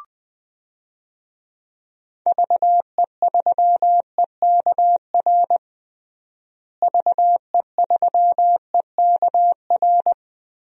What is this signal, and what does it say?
Morse code, heard as a keyed tone, about 19 wpm, 700 Hz. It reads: VE3EKR VE3EKR
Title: DXCC-100 Call Signs - 6 Characters - Repeated 3x - 20wpm